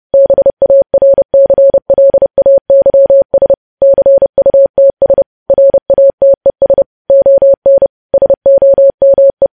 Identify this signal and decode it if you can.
Morse code, heard as a keyed tone, about 30 wpm, 550 Hz. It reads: BARCLAYS CUTS RATES ON SOME
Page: news headlines in morse code at 30 wpm